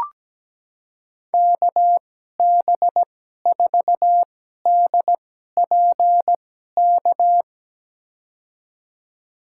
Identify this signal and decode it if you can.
Morse code, heard as a keyed tone, about 17 wpm, spaced as for 12 wpm, 700 Hz. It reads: KB4DPK
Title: USA Call Signs - 6 Characters - 2x Character Spacing - 17wpm